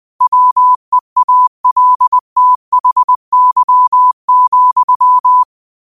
Morse code, heard as a keyed tone, about 20 wpm, 1000 Hz. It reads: WEALTHY,
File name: qotdfiles / 20wpm